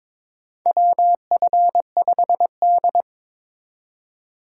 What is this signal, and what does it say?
Morse code, heard as a keyed tone, about 22 wpm, 700 Hz. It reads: WF5D
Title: USA Call Signs - 4 Characters - 3x Character Spacing - 22wpm